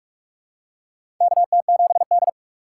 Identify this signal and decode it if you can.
Morse code, heard as a keyed tone, about 42 wpm, 700 Hz. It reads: KT7D
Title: USA Call Signs - 4 Characters - 4x Character Spacing - 45wpm